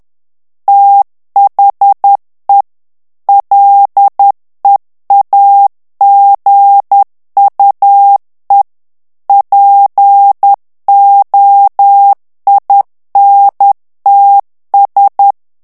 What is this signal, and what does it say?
Morse code, heard as a keyed tone, about 10 wpm, 800 Hz. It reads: THE LEAGUE POINTS
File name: ANGLAIS texte clair 3 vitesse 10 mots minute